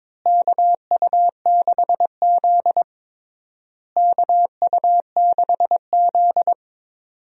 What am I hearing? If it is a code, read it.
Morse code, heard as a keyed tone, about 22 wpm, 700 Hz. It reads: KU6Z KU6Z
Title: USA Call Signs - 4 Characters - Repeated 3x - 22wpm